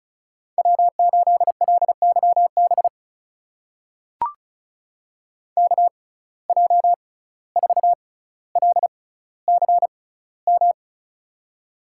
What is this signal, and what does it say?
Morse code, heard as a keyed tone, about 36 wpm, spaced as for 17 wpm, 700 Hz. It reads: W8LYB KJ4LCM
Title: USA Call Signs - 6x Character Spacing - 35wpm